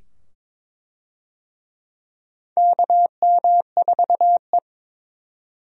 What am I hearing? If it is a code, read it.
Morse code, heard as a keyed tone, about 22 wpm, 700 Hz. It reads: KM4E
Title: USA Call Signs - 4 Characters - 22wpm